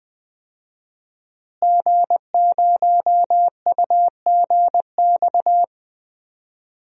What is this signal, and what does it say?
Morse code, heard as a keyed tone, about 20 wpm, 700 Hz. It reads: G0UGX
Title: DXCC-100 Call Signs - 5 Characters - Repeated 2x - 20wpm